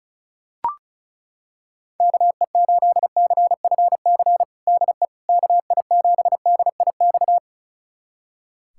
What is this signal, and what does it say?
Morse code, heard as a keyed tone, about 36 wpm, 700 Hz. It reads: KE8CFC DE KI7DIX
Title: USA Callsign DE Callsign - 6 Characters - 35wpm